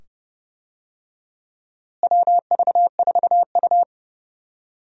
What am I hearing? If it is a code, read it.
Morse code, heard as a keyed tone, about 30 wpm, 700 Hz. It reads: WV4U
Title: USA Call Signs - 4 Characters - Repeated 2x - 30wpm